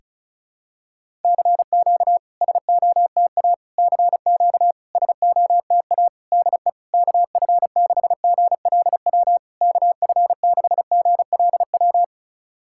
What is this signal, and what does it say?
Morse code, heard as a keyed tone, about 36 wpm, 700 Hz. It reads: CQ SOTA CQ SOTA DE KF6GLW KF6GLW
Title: SOTA QSO Practice 35wpm